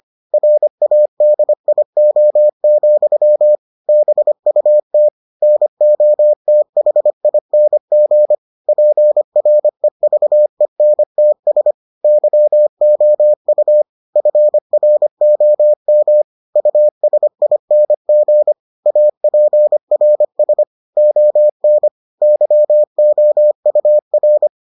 Morse code, heard as a keyed tone, about 25 wpm, 600 Hz. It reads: RADIO, BUT NOTHING PREVENTS YOU FROM USING APRS ON YOUR